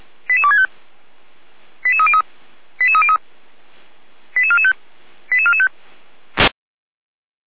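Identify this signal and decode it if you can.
Five-tone selective call (ZVEI), heard as quick runs of five tones, tones of about 70 ms. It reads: 88265 88303 88303 88445 88445
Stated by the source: ZVEI five-tone call received on SDR, channel 455.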